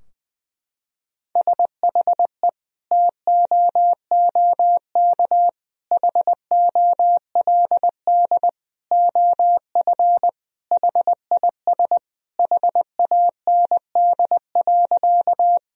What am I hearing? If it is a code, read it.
Morse code, heard as a keyed tone, about 20 wpm, 700 Hz. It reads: SHE TOOK HOLD OF HIS HAND.
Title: Sentences from Top 1000 Words - Review Part 2 - 20wpm